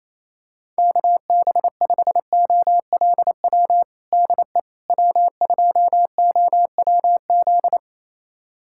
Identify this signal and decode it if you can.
Morse code, heard as a keyed tone, about 28 wpm, 700 Hz. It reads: KB5OLW DE W2OWZ
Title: USA Callsign DE Callsign 28wpm